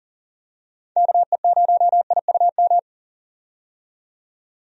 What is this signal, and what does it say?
Morse code, heard as a keyed tone, about 40 wpm, 700 Hz. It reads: KE0IUM
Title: USA Call Signs - 6 Characters - 40wpm